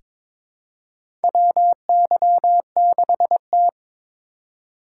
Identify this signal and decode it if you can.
Morse code, heard as a keyed tone, about 23 wpm, 700 Hz. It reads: WY6T
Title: USA Call Signs - 4 Characters - 4x Character Spacing - 22wpm